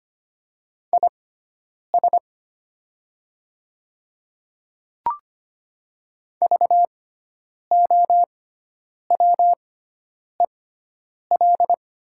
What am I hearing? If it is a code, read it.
Morse code, heard as a keyed tone, about 25 wpm, spaced as for 9 wpm, 700 Hz. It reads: IS VOWEL
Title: 6x Character Spacing - Top 500 Words - 25wpm